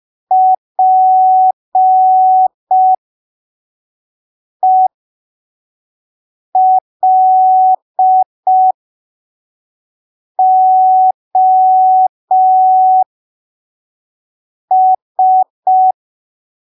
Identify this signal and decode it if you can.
Morse code, heard as a keyed tone, about 5 wpm, spaced as for 3 wpm, 750 Hz. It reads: PELOS
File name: arquivos / texto / texto 06 05ppm